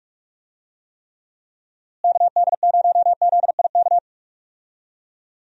Morse code, heard as a keyed tone, about 46 wpm, 700 Hz. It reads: KD0ZIK
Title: USA Call Signs - 6 Characters - Repeated 3x - 45wpm